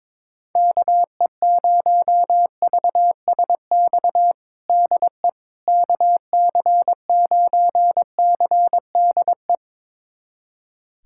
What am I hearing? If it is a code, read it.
Morse code, heard as a keyed tone, about 22 wpm, 700 Hz. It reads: KE0VSX DE KC9CDE
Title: USA Callsign DE Callsign 22wpm